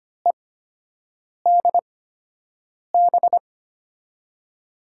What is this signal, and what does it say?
Morse code, heard as a keyed tone, about 25 wpm, spaced as for 7 wpm, 700 Hz. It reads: EDB
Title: USA Call Signs - 5 Characters - 8x Character Spacing - 25wpm